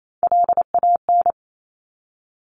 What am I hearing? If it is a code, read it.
Morse code, heard as a keyed tone, about 28 wpm, 700 Hz. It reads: LAN
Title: Top Three Letter Combinations in Words 28wpm